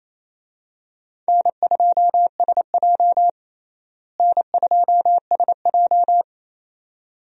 Morse code, heard as a keyed tone, about 27 wpm, 700 Hz. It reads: N2SJ N2SJ
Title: USA Call Signs - 4 Characters - Repeated 2x - 28wpm